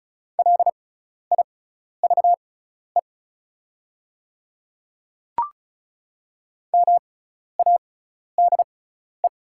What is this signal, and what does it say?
Morse code, heard as a keyed tone, about 36 wpm, spaced as for 12 wpm, 700 Hz. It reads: LIVE MADE
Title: 6x Character Spacing - Top 300 Words - 35wpm